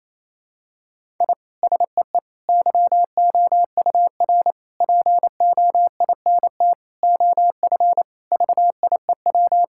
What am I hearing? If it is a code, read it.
Morse code, heard as a keyed tone, about 27 wpm, 700 Hz. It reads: I SEE YOUR POINT OF VIEW
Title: Sentences from Top 1000 Words - Review Part 6 - 28wpm